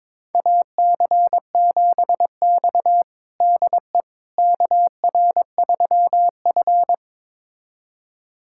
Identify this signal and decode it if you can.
Morse code, heard as a keyed tone, about 22 wpm, 700 Hz. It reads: AC7X DE KR3F
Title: USA Callsign DE Callsign - 4 Characters - 22wpm